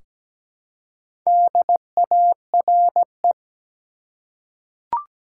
Morse code, heard as a keyed tone, about 17 wpm, 700 Hz. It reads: DARE
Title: Four Letter Words - 17wpm